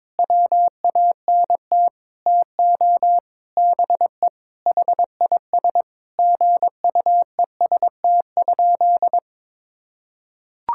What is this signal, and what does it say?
Morse code, heard as a keyed tone, about 22 wpm, 700 Hz. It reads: WANT TO BE HIS GUEST?
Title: Sentences from Top 1600 Words 22wpm